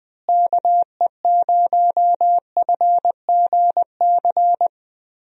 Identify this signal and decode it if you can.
Morse code, heard as a keyed tone, about 20 wpm, 700 Hz. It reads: KE0FGC
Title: USA Call Signs - 6 Characters - Repeated 3x - 20wpm